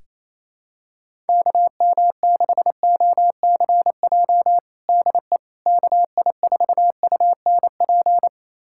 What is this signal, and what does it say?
Morse code, heard as a keyed tone, about 28 wpm, 700 Hz. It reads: KM6OCJ DE KI4UNP
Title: USA Callsign DE Callsign - 6 Characters - 28wpm